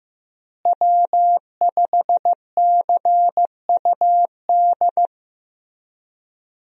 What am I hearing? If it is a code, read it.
Morse code, heard as a keyed tone, about 15 wpm, 700 Hz. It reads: W5CUD
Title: USA Call Signs - 5 Characters - Repeated 2x - 15wpm